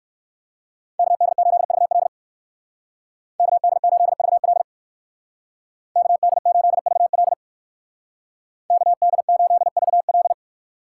Morse code, heard as a keyed tone, about 59 wpm, 700 Hz. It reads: KD8VL KD8VL KD8VL KD8VL